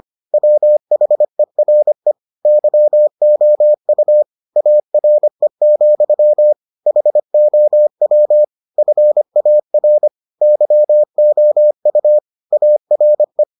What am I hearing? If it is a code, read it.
Morse code, heard as a keyed tone, about 25 wpm, 600 Hz. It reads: WHERE YOU ARE, HOW FAR YOU ARE